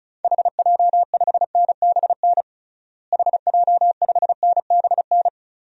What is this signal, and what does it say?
Morse code, heard as a keyed tone, about 36 wpm, 700 Hz. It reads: HJ5NBN HJ5NBN
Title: DXCC-100 Call Signs - 6 Characters - Repeated 2x - 35wpm